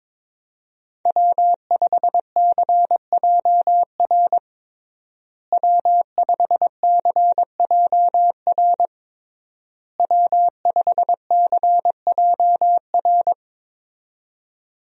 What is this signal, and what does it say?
Morse code, heard as a keyed tone, about 21 wpm, 700 Hz. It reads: W5CJR W5CJR W5CJR
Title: USA Call Signs - 5 Characters - Repeated 3x - 22wpm